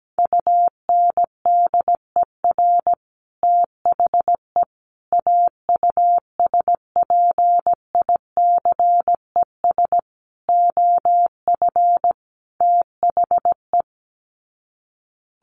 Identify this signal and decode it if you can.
Morse code, heard as a keyed tone, about 17 wpm, 700 Hz. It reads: UNDER THE AUSPICES OF THE
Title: Sets of 5 Words Encore2 17wpm